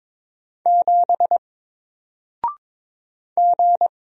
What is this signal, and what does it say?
Morse code, heard as a keyed tone, about 22 wpm, 700 Hz. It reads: 7 G
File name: Single-Letter-Number-22wpm